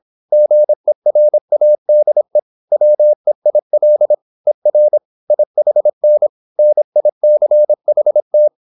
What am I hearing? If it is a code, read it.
Morse code, heard as a keyed tone, about 26 wpm, 600 Hz. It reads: GERADE WEIL ER IHN NICHT